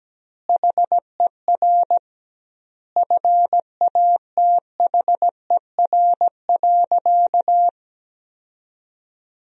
Morse code, heard as a keyed tone, about 17 wpm, 700 Hz. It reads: HER FATHER.
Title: 2x Word-Spacing - Sentences from Top 300 Words - 17wpm